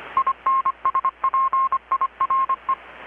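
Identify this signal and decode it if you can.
Morse code, heard as a keyed tone, about 25 wpm, 1070 Hz, with radio noise. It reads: INSPIRE
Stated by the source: Morse from KPH's Night of Nights XX maritime radiotelegraphy event, received in Chiapas, Mexico about 3,800 km away on a Kenwood R-600 with a grounded 10 m long-wire antenna.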